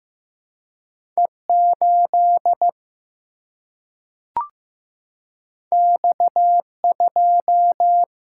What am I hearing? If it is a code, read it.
Morse code, heard as a keyed tone, about 15 wpm, 700 Hz. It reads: E 8 X 2
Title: Single-Letter Single-Number 15wpm